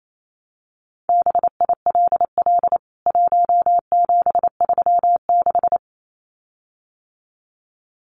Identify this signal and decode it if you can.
Morse code, heard as a keyed tone, about 28 wpm, 700 Hz. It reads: BILL 1736
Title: CWT Exchange 28wpm